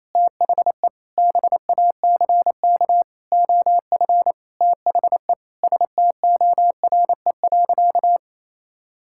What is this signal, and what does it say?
Morse code, heard as a keyed tone, about 28 wpm, 700 Hz. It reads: THE BACK OF THE STORE.
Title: Sentences from Top 1000 Words - Review Part 5 - 28wpm